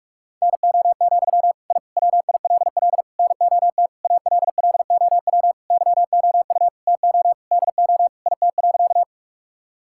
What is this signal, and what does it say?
Morse code, heard as a keyed tone, about 41 wpm, 700 Hz. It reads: NO, I WILL NOT ALLOW YOU TO DO IT.
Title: Sentences from Top 1000 Words - Review Part 5 - 45wpm